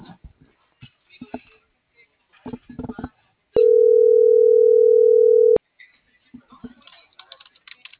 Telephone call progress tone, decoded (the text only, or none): ringback tone